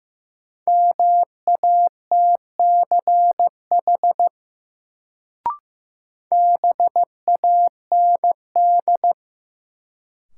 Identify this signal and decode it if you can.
Morse code, heard as a keyed tone, about 15 wpm, 700 Hz. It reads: MATCH BAND
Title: Sentences from Top 1600 Words - New Word Drill Alt - 15wpm